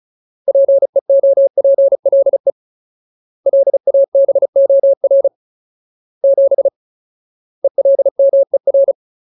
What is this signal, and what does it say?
Morse code, heard as a keyed tone, about 34 wpm, 550 Hz. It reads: PEOPLE LABOR 7 ELMER
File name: Ultimate Word List / 35WPM Ultimate